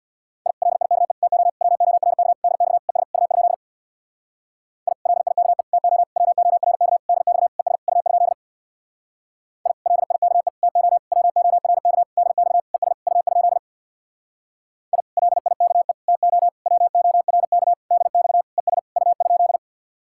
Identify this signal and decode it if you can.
Morse code, heard as a keyed tone, about 64 wpm, 700 Hz. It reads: I LIKE TO WORK DX ES U? I LIKE TO WORK DX ES U? I LIKE TO WORK DX ES U? I LIKE TO WORK DX ES U?